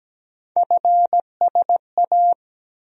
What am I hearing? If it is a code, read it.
Morse code, heard as a keyed tone, about 17 wpm, 700 Hz. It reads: FSA
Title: USA Call Sign Postfixes - 3 Letters 17wpm